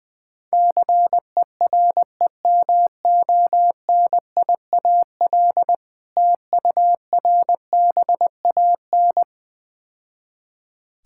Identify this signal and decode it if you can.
Morse code, heard as a keyed tone, about 20 wpm, 700 Hz. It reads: CEREMONIAL TURBAN
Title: Difficult Words - Speed-Racing with Hints - 20wpm